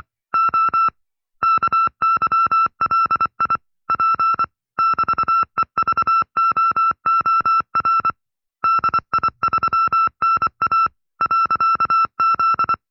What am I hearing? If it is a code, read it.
Morse code, heard as a keyed tone, about 24 wpm, 1340 Hz. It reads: O KYLI P <BT>EVOOR DI3NA .Z
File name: FreeFiles / CW-practice / 20WPM-side2